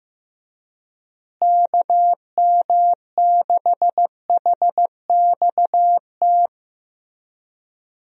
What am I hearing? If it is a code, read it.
Morse code, heard as a keyed tone, about 16 wpm, 700 Hz. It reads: KM6HXT
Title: USA Call Signs - 6 Characters - Repeated 2x - 15wpm